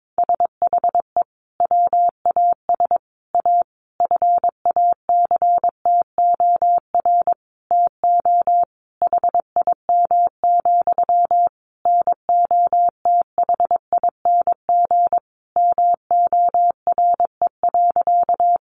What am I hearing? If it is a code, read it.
Morse code, heard as a keyed tone, about 22 wpm, 700 Hz. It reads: SHE WAS A FACTOR TO HIM, NOTHING MORE.
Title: Sentences from Top 1300 Words 22wpm